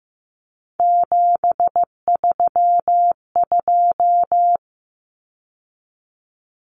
Morse code, heard as a keyed tone, about 15 wpm, 700 Hz. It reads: 732
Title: Numbers - 3-Digit 15wpm